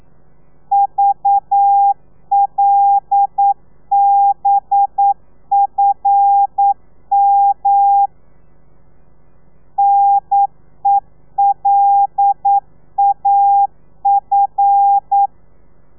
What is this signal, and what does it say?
Morse code, heard as a keyed tone, about 9 wpm, 800 Hz. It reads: VLBFM NELAF
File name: telegrafia / i2tfj / Lez5 40